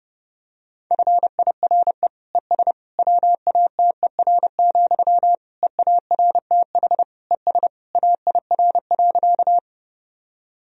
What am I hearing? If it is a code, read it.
Morse code, heard as a keyed tone, about 30 wpm, 700 Hz. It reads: FIRE ES WATER, EARTH ES AIR.
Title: Sentences from Top 400 Words and Abbreviations 30wpm